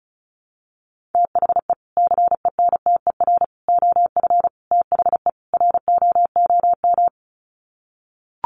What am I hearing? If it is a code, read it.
Morse code, heard as a keyed tone, about 34 wpm, 700 Hz. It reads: THE CENTER OF THE ROOM
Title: Sets of 5 Words Encore 35wpm